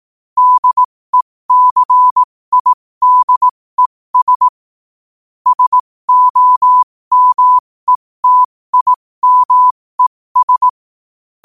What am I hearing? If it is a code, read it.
Morse code, heard as a keyed tone, about 18 wpm, spaced as for 15 wpm, 1000 Hz. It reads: DECIDES SOMETIMES